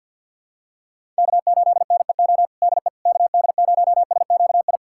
Morse code, heard as a keyed tone, about 52 wpm, 700 Hz. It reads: K8NEO DE KD0SQI
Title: USA Callsign DE Callsign 50wpm